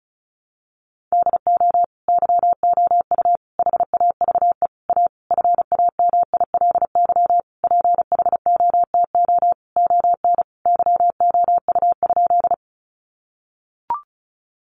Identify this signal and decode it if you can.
Morse code, heard as a keyed tone, about 34 wpm, 700 Hz. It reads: DO YOU HAVE A FAMILY PHOTO ON YOU?
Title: Sentences from Top 1200 Words 35wpm